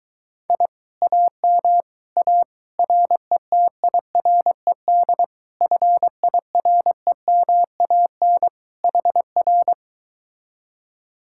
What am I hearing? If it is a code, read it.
Morse code, heard as a keyed tone, about 22 wpm, 700 Hz. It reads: I AM A RETIRED FIREMAN HR